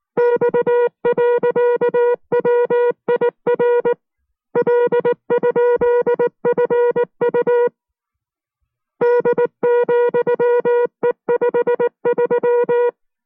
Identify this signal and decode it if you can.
Morse code, heard as a keyed tone, about 19 wpm, 480 Hz. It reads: X.WIR L?FU D,E53